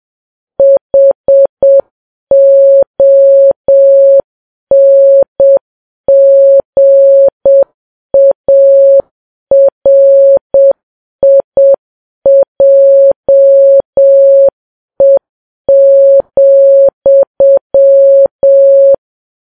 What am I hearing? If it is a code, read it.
Morse code, heard as a keyed tone, about 7 wpm, 550 Hz. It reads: HONGARIJE,
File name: Files Web tls / Morse on9fv / morse7kps